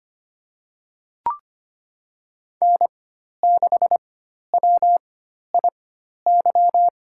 Morse code, heard as a keyed tone, about 25 wpm, spaced as for 12 wpm, 700 Hz. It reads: N6WIY
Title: USA Call Signs - 4x Character Spacing - 25wpm